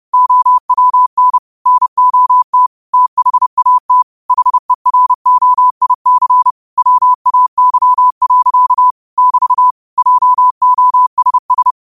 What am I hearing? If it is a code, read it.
Morse code, heard as a keyed tone, about 31 wpm, 1000 Hz. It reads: OWN NOT THAT HEROIC WAY. <BT> JOSS